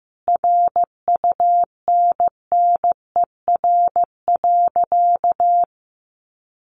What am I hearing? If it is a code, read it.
Morse code, heard as a keyed tone, about 15 wpm, 700 Hz. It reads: RUNNER.
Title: Additional Easy Phrases 15wpm